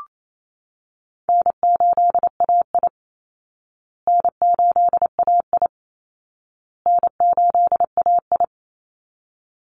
Morse code, heard as a keyed tone, about 28 wpm, 700 Hz. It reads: N8AI N8AI N8AI
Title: Easy Call Signs Repeated 3x - 28wpm